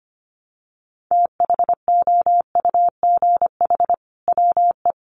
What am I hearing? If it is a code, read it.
Morse code, heard as a keyed tone, about 25 wpm, 700 Hz. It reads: THOUGH WE
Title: Sentences from Top 1000 Words 25wpm